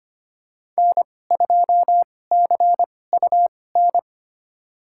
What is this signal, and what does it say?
Morse code, heard as a keyed tone, about 25 wpm, spaced as for 18 wpm, 700 Hz. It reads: N2CUN
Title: USA Call Signs - 2x Character Spacing - 25wpm